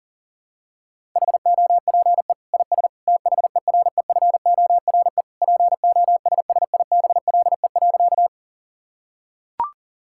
Morse code, heard as a keyed tone, about 40 wpm, 700 Hz. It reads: HOPE IS THEREFORE POSSIBLE.
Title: Sentences from Top 1000 Words - Review Part 5 - 40wpm